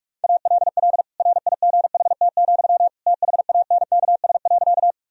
Morse code, heard as a keyed tone, about 46 wpm, 700 Hz. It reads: ALL RIGHT, THANKS.
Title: Sentences from Top 1000 Words - Review Part 3 - 45wpm